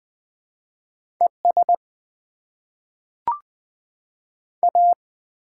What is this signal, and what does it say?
Morse code, heard as a keyed tone, about 21 wpm, spaced as for 5 wpm, 700 Hz. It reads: ESA
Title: Top 20 QSO Elements - 6x Character Spacing - 20wpm